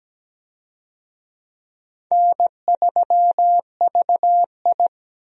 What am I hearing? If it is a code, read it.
Morse code, heard as a keyed tone, about 17 wpm, 700 Hz. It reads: N3VI
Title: USA Call Signs - 4 Characters - 17wpm